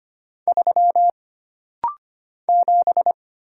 Morse code, heard as a keyed tone, about 25 wpm, 700 Hz. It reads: 3 7
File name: Single Character (Letter:Number:Punctuation:Prosign) - Rapid-Fire - 25wpm